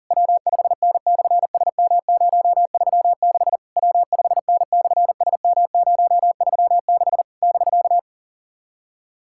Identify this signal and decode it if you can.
Morse code, heard as a keyed tone, about 40 wpm, 700 Hz. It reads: W5N/SM036 W5N/SM036 <BK>
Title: SOTA QSO Practice 40wpm